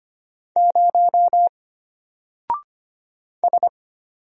Morse code, heard as a keyed tone, about 25 wpm, 700 Hz. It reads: 0 S
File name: Single-Letter-Number-25wpm